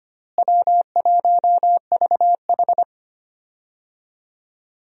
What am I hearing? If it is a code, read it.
Morse code, heard as a keyed tone, about 25 wpm, 700 Hz. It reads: W1VH
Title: USA Call Signs - 4 Characters - Repeated 3x - 25wpm